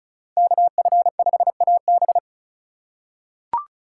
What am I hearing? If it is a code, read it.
Morse code, heard as a keyed tone, about 36 wpm, 700 Hz. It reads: KF5AB
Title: USA Call Signs - 5 Characters - 2x Character Spacing - 35wpm